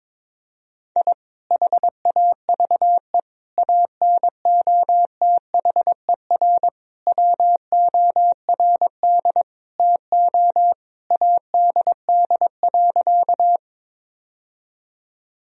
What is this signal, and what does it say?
Morse code, heard as a keyed tone, about 22 wpm, 700 Hz. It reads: I HAVE ANOTHER WORD TO ADD.
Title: Sentences from Top 1000 Words - Review Part 1 - 22wpm